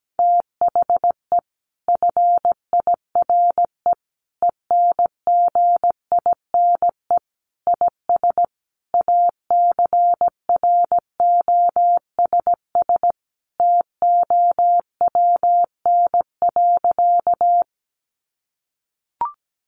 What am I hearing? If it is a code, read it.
Morse code, heard as a keyed tone, about 17 wpm, 700 Hz. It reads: THE FIRE ENGINE IS ACROSS TOWN.
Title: Sentences from Top 1500 Words 17wpm